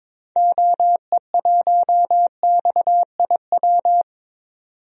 Morse code, heard as a keyed tone, about 22 wpm, 700 Hz. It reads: OE1XIW
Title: DXCC-100 Call Signs - 6 Characters - 22wpm